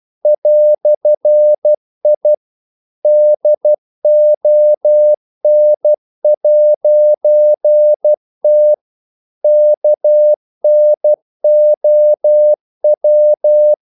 Morse code, heard as a keyed tone, about 12 wpm, 600 Hz. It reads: "I DON'T KNOW
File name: CW / Practice / EnglishQuotations0312